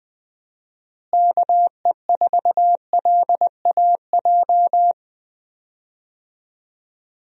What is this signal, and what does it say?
Morse code, heard as a keyed tone, about 20 wpm, 700 Hz. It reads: KE4LAJ
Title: USA Call Signs - 6 Characters - Repeated 3x - 20wpm